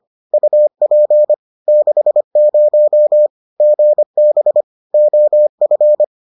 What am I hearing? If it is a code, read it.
Morse code, heard as a keyed tone, about 25 wpm, 600 Hz. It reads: UP 60 GB OF